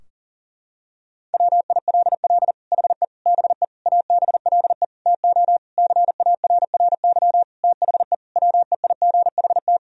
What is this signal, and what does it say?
Morse code, heard as a keyed tone, about 40 wpm, 700 Hz. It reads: WILL HE BE ABLE TO CARRY THE WEIGHT
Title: Sentences from Top 1000 Words - Review Part 3 - 40wpm